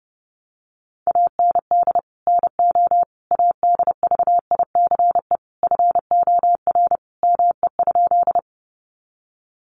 Morse code, heard as a keyed tone, about 30 wpm, 700 Hz. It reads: AND NO ADVICE FOR ME?
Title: Sentences from Top 1100 Words 30wpm